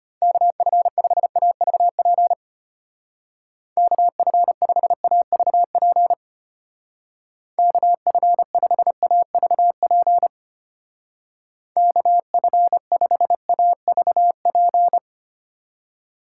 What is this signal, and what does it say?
Morse code, heard as a keyed tone, about 32 wpm, 700 Hz. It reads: KF5AVP KF5AVP KF5AVP KF5AVP